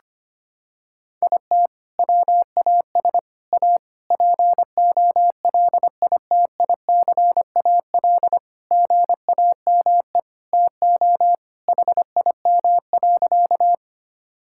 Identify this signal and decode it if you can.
Morse code, heard as a keyed tone, about 25 wpm, 700 Hz. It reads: IT WAS A POLITICAL GAME TO HIM.
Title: Sentences from Top 1000 Words - Review Part 5 - 25wpm